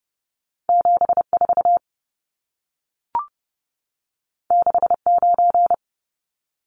Morse code, heard as a keyed tone, about 30 wpm, 700 Hz. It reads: 74 69
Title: Numbers - 2-Digit 30wpm